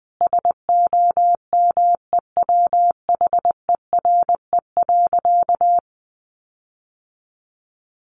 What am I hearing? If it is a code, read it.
Morse code, heard as a keyed tone, about 20 wpm, 700 Hz. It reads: SOMEWHERE.
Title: Sentences from Top 1400 Words 20wpm